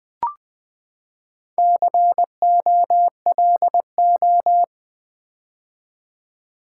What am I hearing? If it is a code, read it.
Morse code, heard as a keyed tone, about 20 wpm, 700 Hz. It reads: COLO
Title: Build A Word - 4 Letters At A Time - 8-Letter Words - 20wpm